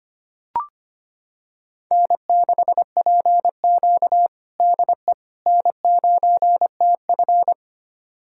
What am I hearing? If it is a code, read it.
Morse code, heard as a keyed tone, about 25 wpm, 700 Hz. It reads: N6PQ DE N9TF
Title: USA Callsign DE Callsign - 4 Characters - 25wpm